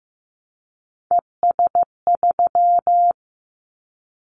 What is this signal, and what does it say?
Morse code, heard as a keyed tone, about 15 wpm, 700 Hz. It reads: ES3
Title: Callsign Prefixes for Top 50 DXCC - 15wpm